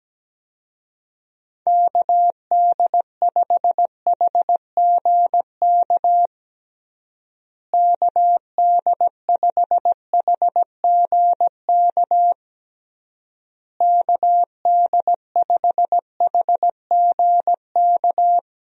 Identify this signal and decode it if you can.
Morse code, heard as a keyed tone, about 17 wpm, 700 Hz. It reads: KD5HGK KD5HGK KD5HGK
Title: USA Call Signs - 6 Characters - Repeated 3x - 17wpm